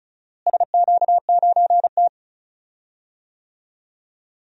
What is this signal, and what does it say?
Morse code, heard as a keyed tone, about 34 wpm, 700 Hz. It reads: SQ9T
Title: DXCC-100 Call Signs - 4 Characters - Repeated 2x - 35wpm